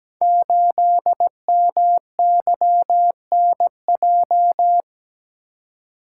Morse code, heard as a keyed tone, about 17 wpm, 700 Hz. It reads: 8MYNJ
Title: Code Groups - 5 Letter:Numbers - 17wpm